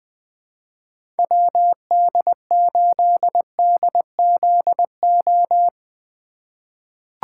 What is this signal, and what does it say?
Morse code, heard as a keyed tone, about 20 wpm, 700 Hz. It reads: WD8DZO
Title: USA Call Signs - 4x Character Spacing - 20wpm